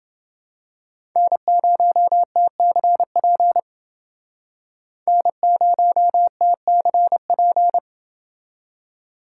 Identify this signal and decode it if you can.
Morse code, heard as a keyed tone, about 29 wpm, 700 Hz. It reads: N0TCP N0TCP